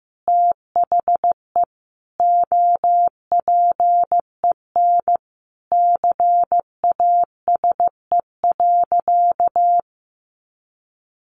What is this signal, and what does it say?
Morse code, heard as a keyed tone, about 15 wpm, 700 Hz. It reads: THE OPEN CASE.
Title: Sentences from Top 1000 Words 15wpm